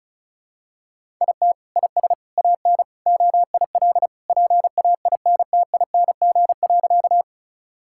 Morse code, heard as a keyed tone, about 36 wpm, 700 Hz. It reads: IT IS AN OIL PAINTING.
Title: Sentences from Top 1000 Words - Review Part 6 - 35wpm